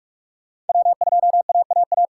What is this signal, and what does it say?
Morse code, heard as a keyed tone, about 41 wpm, 700 Hz. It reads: W2AAA
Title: USA Call Signs - 2x Character Spacing - 45wpm